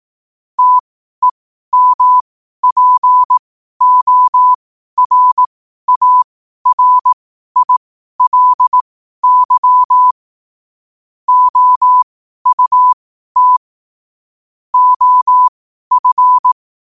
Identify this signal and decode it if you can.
Morse code, heard as a keyed tone, about 18 wpm, spaced as for 12 wpm, 1000 Hz. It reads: TEMPORARILY OUT OF
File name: qotdfiles / 12wpm